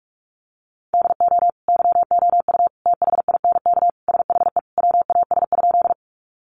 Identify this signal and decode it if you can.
Morse code, heard as a keyed tone, about 44 wpm, 700 Hz. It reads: DO YOU THINK SHE WAS?
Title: Sentences from Top 200 Words 45wpm